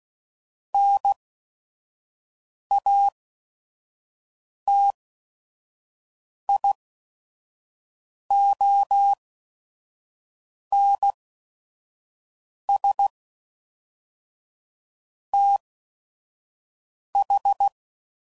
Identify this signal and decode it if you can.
Morse code, heard as a keyed tone, about 16 wpm, spaced as for 5 wpm, 800 Hz. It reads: NATIONS TH
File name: CW Code Groups @5WPM / Technical Subjects / AND Gate